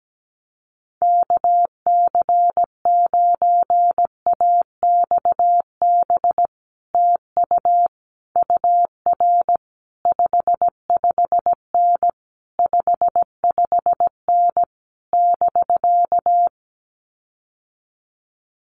Morse code, heard as a keyed tone, about 17 wpm, 700 Hz. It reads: KC9AXB TU UR 55N 55N <BK>
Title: Common QSO Phrases Encore 17wpm